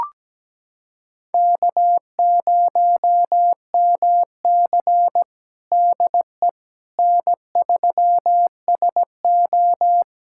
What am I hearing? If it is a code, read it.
Morse code, heard as a keyed tone, about 17 wpm, 700 Hz. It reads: K0MC DE N3SO
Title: USA Callsign DE Callsign - 4 Characters - 17wpm